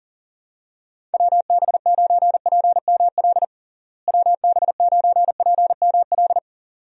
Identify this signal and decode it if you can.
Morse code, heard as a keyed tone, about 37 wpm, 700 Hz. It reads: WB9PML WB9PML
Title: USA Call Signs - 6 Characters - Repeated 2x - 40wpm